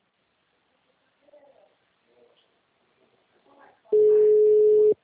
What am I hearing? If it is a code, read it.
Ringback tone (Europe, 425 Hz).
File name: out- -2030- - - .46335